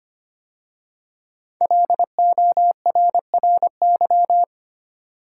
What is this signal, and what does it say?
Morse code, heard as a keyed tone, about 25 wpm, 700 Hz. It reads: LORRY
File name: 5 Letter Words - First 3 Letters Given - 25wpm